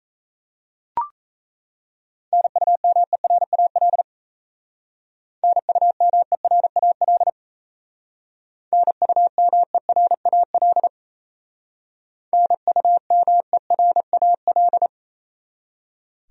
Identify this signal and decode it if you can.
Morse code, heard as a keyed tone, about 33 wpm, 700 Hz. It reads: NUMERAL NUMERAL NUMERAL NUMERAL
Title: Speed-Racing Top 500 Words 28wpm